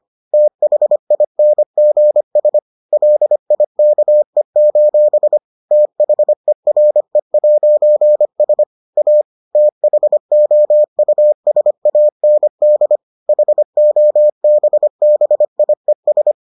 Morse code, heard as a keyed tone, about 25 wpm, 600 Hz. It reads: THINGS LIKE: THERE'S A THOUSAND HOBBIES